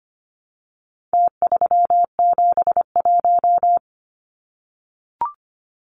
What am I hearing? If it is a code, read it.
Morse code, heard as a keyed tone, about 25 wpm, 700 Hz. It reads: T371
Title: Numbers - Contest Style - 4-Digit 25wpm